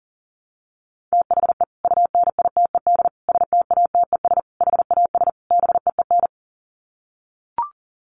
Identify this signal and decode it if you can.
Morse code, heard as a keyed tone, about 40 wpm, 700 Hz. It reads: THE UNITED STATES HAS BEEN
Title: Sets of 5 Words Encore2 40wpm